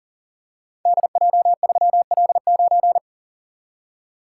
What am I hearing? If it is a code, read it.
Morse code, heard as a keyed tone, about 40 wpm, 700 Hz. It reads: DJ3L9
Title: Code Groups - 5 Letter:Numbers - 40wpm